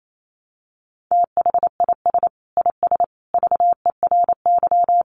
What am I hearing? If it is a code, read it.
Morse code, heard as a keyed tone, about 28 wpm, 700 Hz. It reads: THIS IS VERY
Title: Sentences from Top 1200 Words 28wpm